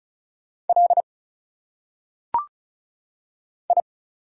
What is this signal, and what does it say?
Morse code, heard as a keyed tone, about 36 wpm, 700 Hz. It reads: L I
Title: Single Letters - Phonetic Pronunciation 35wpm